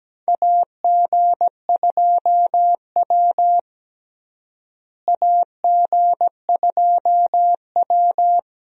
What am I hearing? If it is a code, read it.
Morse code, heard as a keyed tone, about 17 wpm, 700 Hz. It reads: AG2W AG2W
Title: USA Call Signs - 4 Characters - Repeated 3x - 17wpm